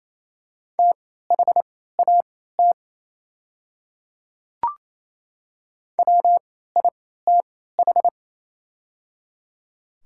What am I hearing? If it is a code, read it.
Morse code, heard as a keyed tone, about 28 wpm, spaced as for 16 wpm, 700 Hz. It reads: THAT WITH
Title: 3x Character Spacing - Top 100 Words - 28wpm